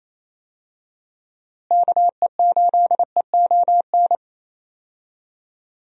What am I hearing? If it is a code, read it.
Morse code, heard as a keyed tone, about 28 wpm, 700 Hz. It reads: KE8EON
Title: USA Call Signs - 6 Characters - Repeated 2x - 28wpm